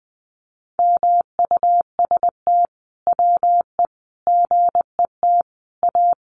Morse code, heard as a keyed tone, about 20 wpm, 700 Hz. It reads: MUST WE GET A
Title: Sentences from Top 1400 Words 20wpm